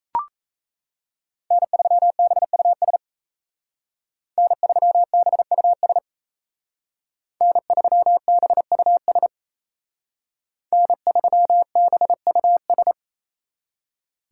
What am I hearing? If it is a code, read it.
Morse code, heard as a keyed tone, about 36 wpm, 700 Hz. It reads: N3BUS N3BUS N3BUS N3BUS